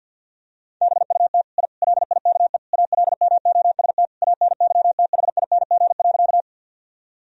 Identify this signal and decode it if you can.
Morse code, heard as a keyed tone, about 50 wpm, 700 Hz. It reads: BUT I LIKE ALMOST ANYTHING.
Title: Sentences from Top 1000 Words - Review Part 4 - 50wpm